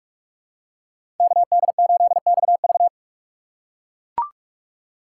Morse code, heard as a keyed tone, about 41 wpm, 700 Hz. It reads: KD8XV
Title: USA Call Signs - 5 Characters - 4x Character Spacing - 45wpm